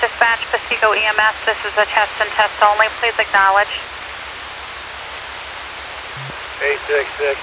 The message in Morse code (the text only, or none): none